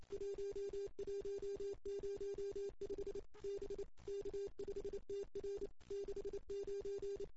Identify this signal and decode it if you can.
Morse code, heard as a keyed tone, about 25 wpm, 440 Hz. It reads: 1105 B K5TR 69
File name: CW / 2008-ss-cw-k5tr-op-n5rz -14:00